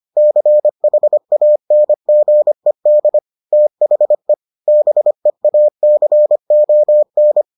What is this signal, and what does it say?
Morse code, heard as a keyed tone, about 25 wpm, 600 Hz. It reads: CHANGED THE BEACON